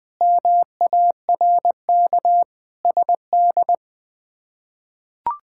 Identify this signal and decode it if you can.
Morse code, heard as a keyed tone, about 21 wpm, 700 Hz. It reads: MARK SD
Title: K1USN SST QSO Practice - 20wpm